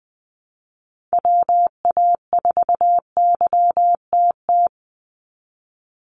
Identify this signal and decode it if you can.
Morse code, heard as a keyed tone, about 20 wpm, 700 Hz. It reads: WA4YTT
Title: USA Call Signs 20wpm